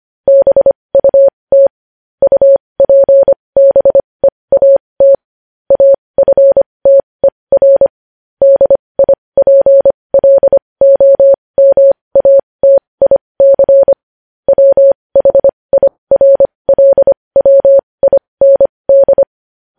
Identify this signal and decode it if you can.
Morse code, heard as a keyed tone, about 25 wpm, spaced as for 20 wpm, 550 Hz. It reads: BUT UPBEAT AFTER DIPLOMATIC WHIRLWIND
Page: news-headlines-in-morse-code-at-20-wpm